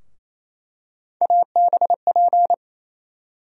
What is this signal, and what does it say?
Morse code, heard as a keyed tone, about 28 wpm, 700 Hz. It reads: ABP
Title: USA Call Sign Postfixes - 3 Letters 28wpm